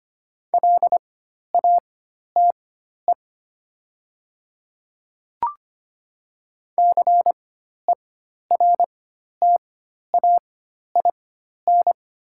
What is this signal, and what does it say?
Morse code, heard as a keyed tone, about 25 wpm, spaced as for 12 wpm, 700 Hz. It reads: LATE CERTAIN
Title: 4x Character Spacing - Top 500 Words - 25wpm